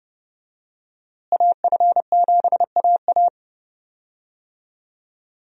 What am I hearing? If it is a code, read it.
Morse code, heard as a keyed tone, about 32 wpm, 700 Hz. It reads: AF7AA
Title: USA Call Signs - 5 Characters - Repeated 3x - 30wpm